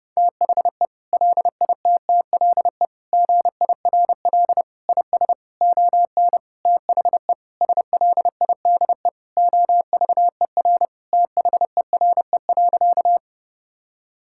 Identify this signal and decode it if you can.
Morse code, heard as a keyed tone, about 30 wpm, 700 Hz. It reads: THE LITTLE GIRL IS ON THE SLIDE OVER THERE.
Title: Sentences from Top 1600 Words 30wpm